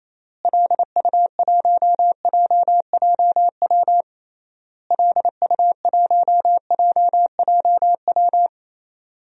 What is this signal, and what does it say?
Morse code, heard as a keyed tone, about 28 wpm, 700 Hz. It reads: LU1JJW LU1JJW
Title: DXCC-100 Call Signs - 6 Characters - Repeated 3x - 28wpm